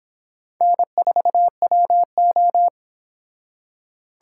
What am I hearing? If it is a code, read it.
Morse code, heard as a keyed tone, about 25 wpm, 700 Hz. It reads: N4WO